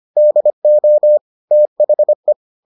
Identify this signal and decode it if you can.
Morse code, heard as a keyed tone, about 25 wpm, 600 Hz. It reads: DO THE